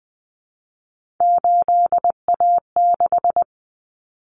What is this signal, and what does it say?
Morse code, heard as a keyed tone, about 20 wpm, 700 Hz. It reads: 8A6
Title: Common Cut Numbers - 3-Digit 20wpm